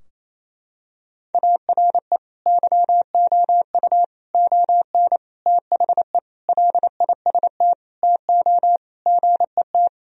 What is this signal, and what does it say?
Morse code, heard as a keyed tone, about 28 wpm, 700 Hz. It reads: ARE YOU ON THE LIST TO GET
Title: Sentences from Top 1000 Words - Review Part 2 - 28wpm